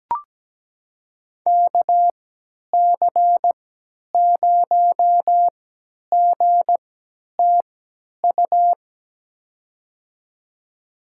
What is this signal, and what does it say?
Morse code, heard as a keyed tone, about 17 wpm, spaced as for 10 wpm, 700 Hz. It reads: KC0GTU
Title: USA Call Signs - 6 Characters - 3x Character Spacing - 17wpm